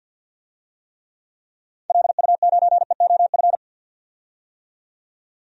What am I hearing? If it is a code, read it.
Morse code, heard as a keyed tone, about 50 wpm, 700 Hz. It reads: LU9EOF
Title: DXCC-100 Call Signs - 6 Characters - 50wpm